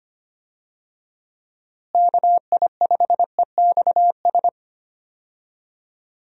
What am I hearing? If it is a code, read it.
Morse code, heard as a keyed tone, about 25 wpm, 700 Hz. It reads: KI5EXS
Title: USA Call Signs - 6 Characters - Repeated 3x - 25wpm